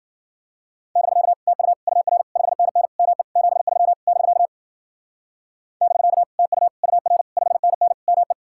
Morse code, heard as a keyed tone, about 51 wpm, 700 Hz. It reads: <BK> TU UR 5NN ME 73 <BK> <BK> TU UR 5NN ME